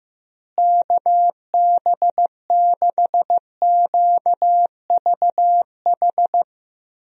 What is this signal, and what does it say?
Morse code, heard as a keyed tone, about 15 wpm, 700 Hz. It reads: KB6QVH
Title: USA Call Signs - 6 Characters - Repeated 3x - 15wpm